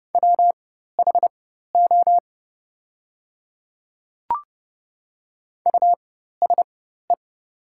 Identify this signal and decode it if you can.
Morse code, heard as a keyed tone, about 30 wpm, spaced as for 14 wpm, 700 Hz. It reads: WHO USE
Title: 4x Character Spacing - Top 200 Words - 30wpm